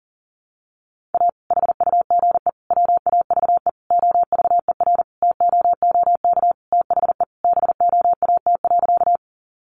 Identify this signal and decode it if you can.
Morse code, heard as a keyed tone, about 37 wpm, 700 Hz. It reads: A HUGE WAVE OVERTOOK THE BOAT.
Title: Sentences from Top 1400 Words 40wpm